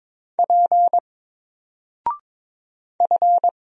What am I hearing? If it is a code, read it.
Morse code, heard as a keyed tone, about 22 wpm, 700 Hz. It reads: P F
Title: Single Character (Letter:Number:Punctuation:Prosign) Alt 22wpm